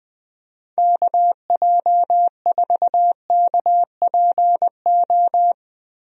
Morse code, heard as a keyed tone, about 20 wpm, 700 Hz. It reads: KJ4KPO
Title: USA Call Signs - 6 Characters - Repeated 2x - 20wpm